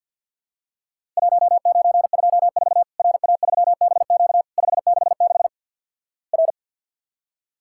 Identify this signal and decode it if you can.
Morse code, heard as a keyed tone, about 50 wpm, 700 Hz. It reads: 1924 RA3BQ 566 R
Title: Worked All Europe DX Contest - QTC Exchange with Repeats - 50wpm